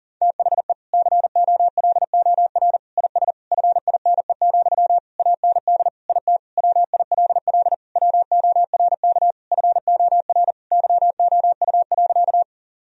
Morse code, heard as a keyed tone, about 40 wpm, 700 Hz. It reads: THE COLOR IS FINE, AND IT WILL WORK FOR YOU.
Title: Sentences from Top 1000 Words - Review Part 2 - 40wpm